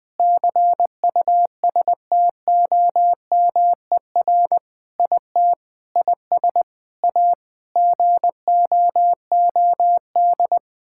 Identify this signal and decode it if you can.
Morse code, heard as a keyed tone, about 20 wpm, 700 Hz. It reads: CUSTOMER IT IS A GOOD
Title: Sentences with 7-Word Phrases - Spoken Beginning - 20wpm